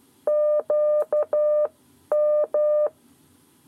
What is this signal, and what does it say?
Morse code, heard as a keyed tone, about 12 wpm, spaced as for 10 wpm, 600 Hz. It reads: QM
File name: 7 Lezione cw 1